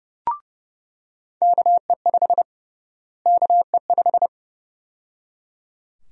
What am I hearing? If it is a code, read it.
Morse code, heard as a keyed tone, about 32 wpm, 700 Hz. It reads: KE5 KE5
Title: USA Call Sign Prefixes - Repeated 2x - 30wpm